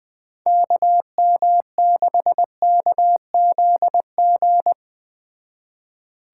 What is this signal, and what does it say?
Morse code, heard as a keyed tone, about 20 wpm, 700 Hz. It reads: KM6KZG
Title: Speed Racing - USA 2x3 Call Signs 17wpm